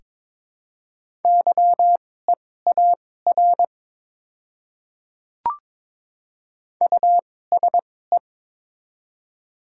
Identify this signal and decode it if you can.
Morse code, heard as a keyed tone, about 22 wpm, spaced as for 16 wpm, 700 Hz. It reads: YEAR USE
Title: 2x Character Spacing - Top 500 Words - 22wpm